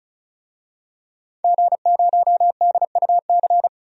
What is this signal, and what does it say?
Morse code, heard as a keyed tone, about 36 wpm, 700 Hz. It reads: G0DUC
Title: DXCC-100 Call Signs - 5 Characters - 35wpm